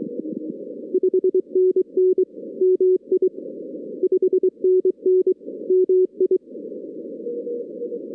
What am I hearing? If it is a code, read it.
Morse code, heard as a keyed tone, about 23 wpm, 370 Hz. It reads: 5NN MI 5NN MI
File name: 2016 arrl dx cw / 2016-ARRL-DX-CW-K5TR- - 0-22